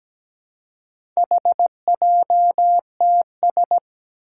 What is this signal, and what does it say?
Morse code, heard as a keyed tone, about 17 wpm, 700 Hz. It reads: HJTS
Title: Four Challenging Letter-Number Combinations 17wpm